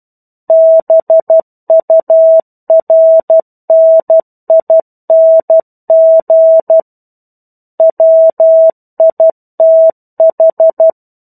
Morse code, heard as a keyed tone, about 12 wpm, 650 Hz. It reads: BURNING WITH
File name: 01-Chapter-01 12 WPM